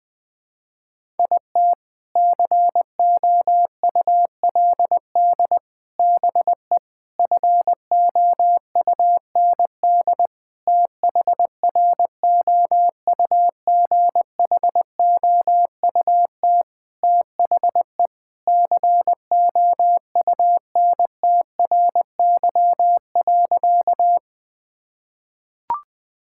Morse code, heard as a keyed tone, about 20 wpm, 700 Hz. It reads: IT COULD BE FOUND THROUGHOUT THE COUNTRY.
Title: Sentences from Top 1000 Words - Review Part 5 - 20wpm